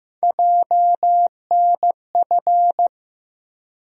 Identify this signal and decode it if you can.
Morse code, heard as a keyed tone, about 15 wpm, 700 Hz. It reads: JNF
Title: USA Call Sign Postfixes - 3 Letters 15wpm